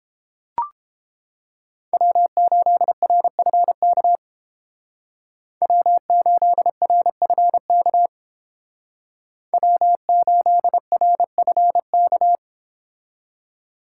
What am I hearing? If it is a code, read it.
Morse code, heard as a keyed tone, about 29 wpm, 700 Hz. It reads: W8RFK W8RFK W8RFK